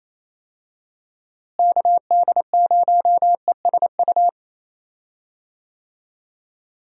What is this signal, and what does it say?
Morse code, heard as a keyed tone, about 27 wpm, 700 Hz. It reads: KD0ESU
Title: USA Call Signs - 6 Characters - Repeated 3x - 28wpm